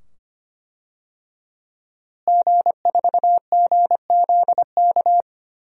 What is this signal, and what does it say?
Morse code, heard as a keyed tone, about 25 wpm, 700 Hz. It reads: G4GZK
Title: DXCC-100 Call Signs - 5 Characters - Repeated 2x - 25wpm